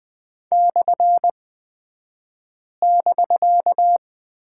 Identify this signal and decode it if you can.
Morse code, heard as a keyed tone, about 20 wpm, spaced as for 5 wpm, 700 Hz. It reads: /<BK>
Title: 249 Lesson 40d - BK - Two Characters 20wpm